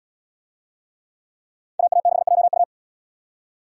Morse code, heard as a keyed tone, about 77 wpm, 700 Hz. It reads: AA6JU